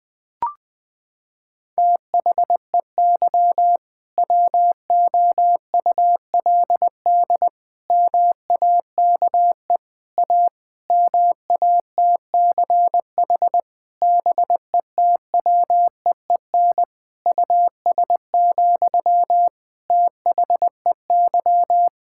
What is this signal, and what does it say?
Morse code, heard as a keyed tone, about 21 wpm, 700 Hz. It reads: THEY WOULD MAKE A MATCH BETWEEN US, THEY
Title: Sentences from Top 1600 Words 20wpm